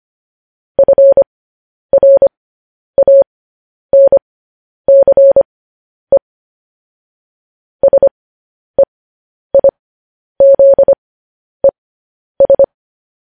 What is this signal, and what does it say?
Morse code, heard as a keyed tone, about 25 wpm, spaced as for 10 wpm, 550 Hz. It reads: FRANCE SEIZES